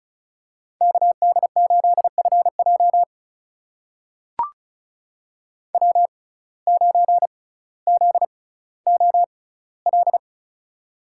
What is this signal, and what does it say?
Morse code, heard as a keyed tone, about 34 wpm, spaced as for 18 wpm, 700 Hz. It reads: KD8FJ W9ZOL
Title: USA Call Signs - 5 Characters - 6x Character Spacing - 35wpm